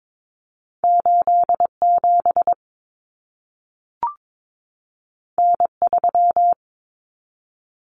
Morse code, heard as a keyed tone, about 22 wpm, 700 Hz. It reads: 87 N3
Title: Common Cut Numbers - 2-Digit 22wpm